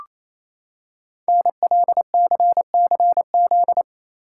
Morse code, heard as a keyed tone, about 28 wpm, 700 Hz. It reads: NLCCZ
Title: Code Groups - 5 Letters - 28wpm